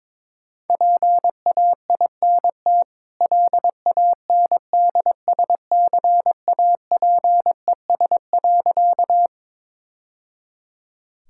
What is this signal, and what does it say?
Morse code, heard as a keyed tone, about 23 wpm, 690 Hz. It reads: PAINT LANDSCAPES.
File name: practice / 22wpm / Easy Sentences with Hints - 22wpm